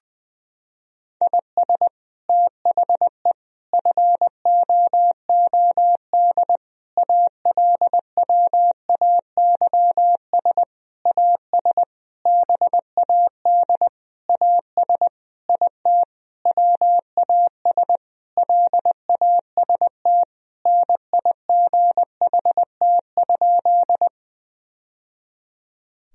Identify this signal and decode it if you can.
Morse code, heard as a keyed tone, about 21 wpm, 700 Hz. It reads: IS THE FOOD ALWAYS AS BAD AS IT WAS LAST NIGHT?
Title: Sentences from Top 1000 Words - Review Part 1 - 20wpm